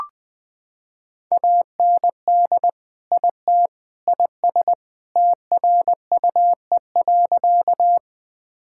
Morse code, heard as a keyed tone, about 20 wpm, 700 Hz. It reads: AND IT IS TRUE.
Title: Sentences with 3-Word Phrases - 20wpm